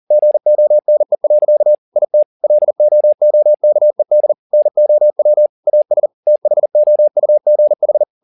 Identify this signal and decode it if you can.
Morse code, heard as a keyed tone, about 40 wpm, 600 Hz. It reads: GONE. IT LOOKED NOW AS THOUGH
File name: the old man and the sea 2 40wpm